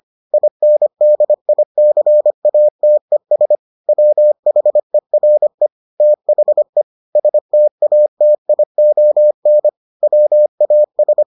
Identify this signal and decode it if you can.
Morse code, heard as a keyed tone, about 25 wpm, 600 Hz. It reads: INDICATES WHERE THE STATION WAS